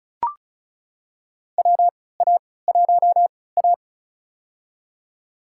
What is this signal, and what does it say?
Morse code, heard as a keyed tone, about 34 wpm, spaced as for 20 wpm, 700 Hz. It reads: WA1A
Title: USA Call Signs - 4 Characters - 3x Character Spacing - 35wpm